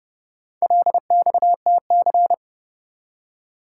Morse code, heard as a keyed tone, about 29 wpm, 700 Hz. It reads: LXTC